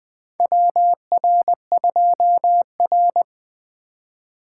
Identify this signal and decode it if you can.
Morse code, heard as a keyed tone, about 20 wpm, 700 Hz. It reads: WR2R
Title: USA Call Signs - 4 Characters - 2x Character Spacing - 20wpm